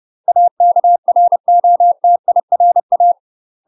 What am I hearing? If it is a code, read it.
Morse code, heard as a keyed tone, about 30 wpm, 700 Hz. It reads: AKROTIRA
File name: UK MASTER QSOs TRAINING / 30WPM / UK MASTER QSOs 0024